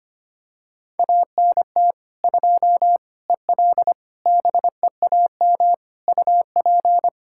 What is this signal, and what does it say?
Morse code, heard as a keyed tone, about 25 wpm, 700 Hz. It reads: ANT 2 EL BEAM UP
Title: K1USN SST QSO Practice 25wpm